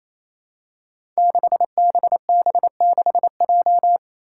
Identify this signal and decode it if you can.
Morse code, heard as a keyed tone, about 28 wpm, 700 Hz. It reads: 6BB6J
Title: Five Challenging Letter-Number Combinations 28wpm